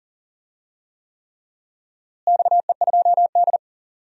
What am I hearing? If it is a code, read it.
Morse code, heard as a keyed tone, about 40 wpm, 700 Hz. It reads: XE2D
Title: DXCC-100 Call Signs - 4 Characters - Repeated 3x - 40wpm